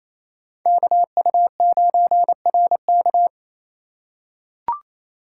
Morse code, heard as a keyed tone, about 28 wpm, 700 Hz. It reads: KU9RK
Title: K1USN SST QSO Practice - 28wpm